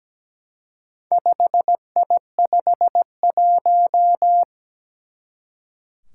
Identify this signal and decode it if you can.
Morse code, heard as a keyed tone, about 17 wpm, 700 Hz. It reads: 5I51
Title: Four Challenging Letter-Number Combinations 17wpm